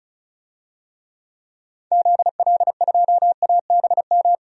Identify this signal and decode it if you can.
Morse code, heard as a keyed tone, about 36 wpm, 700 Hz. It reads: ZL2ABM
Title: DXCC-100 Call Signs - 6 Characters - 35wpm